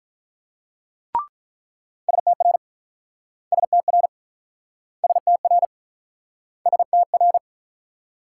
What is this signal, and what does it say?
Morse code, heard as a keyed tone, about 47 wpm, 700 Hz. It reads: STR STR STR STR